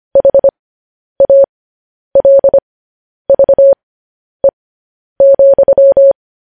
Morse code, heard as a keyed tone, about 25 wpm, spaced as for 10 wpm, 550 Hz. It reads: HALVE,
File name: podcast / News headlines-1x-25x10